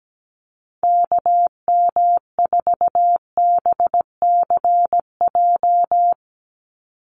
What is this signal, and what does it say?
Morse code, heard as a keyed tone, about 17 wpm, 700 Hz. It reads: KM4BCJ
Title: USA Call Signs 17wpm